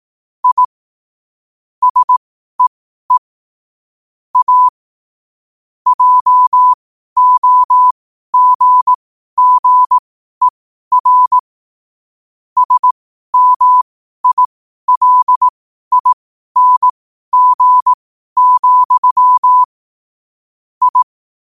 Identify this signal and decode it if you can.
Morse code, heard as a keyed tone, about 18 wpm, spaced as for 12 wpm, 1000 Hz. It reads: I SEE A JOGGER SMILING, I
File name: qotdfiles / 12wpm